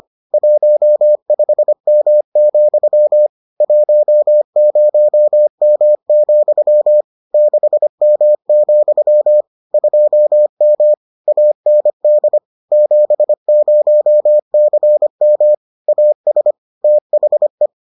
Morse code, heard as a keyed tone, about 25 wpm, 600 Hz. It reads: 15M, 10M, 6M, 2M AND 70CM AS THE